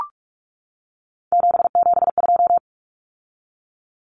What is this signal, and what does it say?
Morse code, heard as a keyed tone, about 44 wpm, 700 Hz. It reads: 772
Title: Numbers - 3-Digit 45wpm